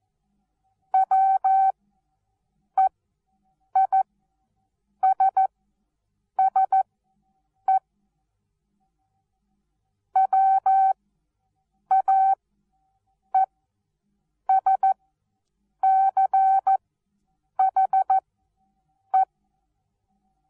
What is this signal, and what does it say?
Morse code, heard as a keyed tone, about 14 wpm, spaced as for 7 wpm, 770 Hz. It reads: WEISSE WAESCHE
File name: Onlinelehrgang / morsen / DJ4UF Morsen19